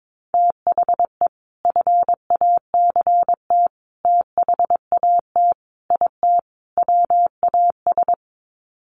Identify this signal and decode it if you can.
Morse code, heard as a keyed tone, about 22 wpm, 700 Hz. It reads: THE FACT THAT IT WAS
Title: Sets of 5 Words Encore2 22wpm